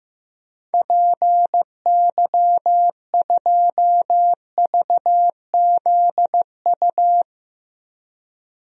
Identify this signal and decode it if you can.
Morse code, heard as a keyed tone, about 15 wpm, 700 Hz. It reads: PY2VZU
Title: DXCC-100 Call Signs - 6 Characters - Repeated 3x - 15wpm